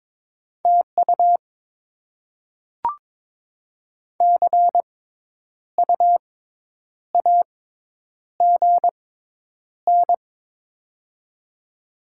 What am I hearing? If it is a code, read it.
Morse code, heard as a keyed tone, about 22 wpm, spaced as for 9 wpm, 700 Hz. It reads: TU CU AGN
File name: Top 20 QSO Elements - 6x Character Spacing - 22wpm